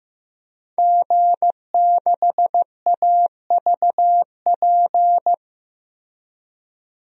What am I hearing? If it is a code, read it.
Morse code, heard as a keyed tone, about 15 wpm, 700 Hz. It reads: G6AVP
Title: DXCC-100 Call Signs - 5 Characters - Repeated 3x - 15wpm